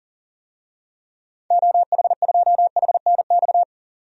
Morse code, heard as a keyed tone, about 40 wpm, 700 Hz. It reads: OH2HNX
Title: DXCC-100 Call Signs - 6 Characters - 40wpm